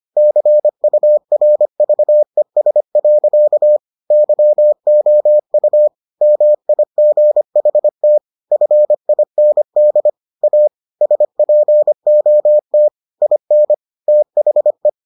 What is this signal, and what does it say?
Morse code, heard as a keyed tone, about 25 wpm, 600 Hz. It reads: CURVES. YOU MIGHT FIND A SPOT IN THE